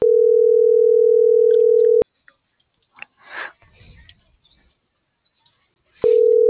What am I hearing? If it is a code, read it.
Ringback tone (North America), 440 + 480 Hz.